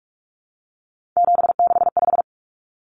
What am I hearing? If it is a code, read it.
Morse code, heard as a keyed tone, about 44 wpm, 700 Hz. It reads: 765
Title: Numbers - 3-Digit 45wpm